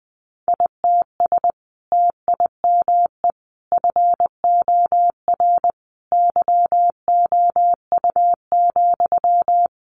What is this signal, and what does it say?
Morse code, heard as a keyed tone, about 20 wpm, 700 Hz. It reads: ITS TIME FOR YOU,
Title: Sentences from Top 600 Words 20wpm